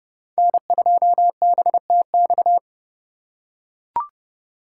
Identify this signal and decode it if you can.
Morse code, heard as a keyed tone, about 30 wpm, 700 Hz. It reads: N2BTX
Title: K1USN SST QSO Practice - 30wpm